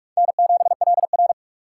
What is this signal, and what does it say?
Morse code, heard as a keyed tone, about 46 wpm, 700 Hz. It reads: N7LR
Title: K1USN SST QSO Practice - 45wpm